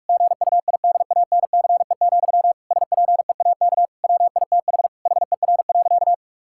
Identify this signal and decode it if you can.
Morse code, heard as a keyed tone, about 42 wpm, 700 Hz. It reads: GUIDANCE, SPEAK WITH HER.
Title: Sentences from Top 1600 Words 45wpm